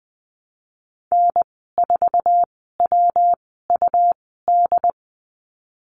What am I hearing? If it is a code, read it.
Morse code, heard as a keyed tone, about 20 wpm, spaced as for 15 wpm, 700 Hz. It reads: N4WUD
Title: 179 Lesson 29g - Slash - Call Signs 20wpm